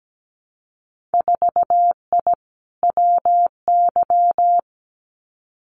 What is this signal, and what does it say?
Morse code, heard as a keyed tone, about 17 wpm, 700 Hz. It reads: 4I WY
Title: Winter Field Day Exchanges Encore 17wpm